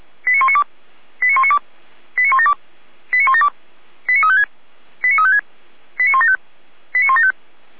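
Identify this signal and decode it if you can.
Five-tone selective call (ZVEI), heard as quick runs of five tones, tones of about 70 ms. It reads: 89192 89192 89172 89172 89356 89356 89175 89175
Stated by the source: SDR-received ZVEI five-tone selective call, from the Feuerwehr Orlamünde audio archive.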